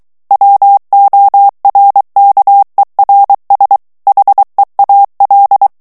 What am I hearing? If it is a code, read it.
Morse code, heard as a keyed tone, about 23 wpm, 800 Hz. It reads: WORKERS HEAL
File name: morse mp3 / ANGLAIS texte clair 1 vitesse 22 mots minute